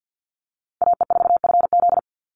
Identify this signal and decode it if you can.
Morse code, heard as a keyed tone, about 48 wpm, 700 Hz. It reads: AE4FZ
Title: NA SOTA Chaser Callsigns 50wpm